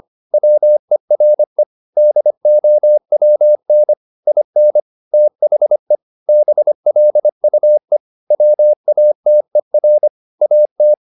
Morse code, heard as a keyed tone, about 25 wpm, 600 Hz. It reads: WERE DOWN IN THE BLUE WATER AT
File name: the old man and the sea part 1 25wpm